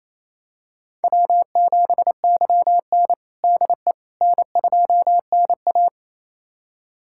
Morse code, heard as a keyed tone, about 28 wpm, 700 Hz. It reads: W7YN DE N2NA
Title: USA Callsign DE Callsign - 4 Characters - 28wpm